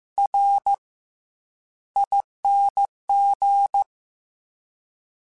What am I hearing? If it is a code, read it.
Morse code, heard as a keyed tone, about 15 wpm, spaced as for 10 wpm, 800 Hz. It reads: RING
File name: Special INGs-mp3s / Special INGs-mp3s / Fast ing @13